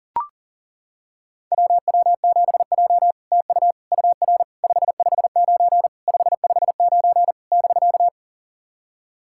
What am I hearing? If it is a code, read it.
Morse code, heard as a keyed tone, about 40 wpm, 700 Hz. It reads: WW7J TU UR 559 559 <BK>
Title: SOTA QSO Practice 40wpm